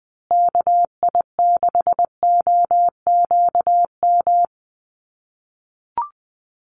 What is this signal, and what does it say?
Morse code, heard as a keyed tone, about 20 wpm, 700 Hz. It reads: KI6OQM
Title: USA Call Signs Repeated 3x 20wpm